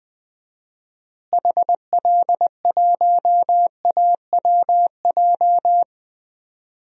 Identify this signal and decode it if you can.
Morse code, heard as a keyed tone, about 20 wpm, 700 Hz. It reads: HL1AWJ
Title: DXCC-100 Call Signs - 6 Characters - Repeated 2x - 20wpm